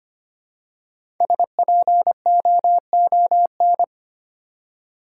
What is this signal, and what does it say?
Morse code, heard as a keyed tone, about 25 wpm, 700 Hz. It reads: SPOON
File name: practice / 25wpm / Easy Words with Hints - 25wpm